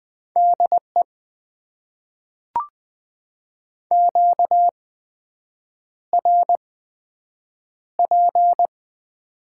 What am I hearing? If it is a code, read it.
Morse code, heard as a keyed tone, about 20 wpm, spaced as for 7 wpm, 700 Hz. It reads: DE QRP
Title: Top 90 QSO Elements - 8x Character Spacing - 20wpm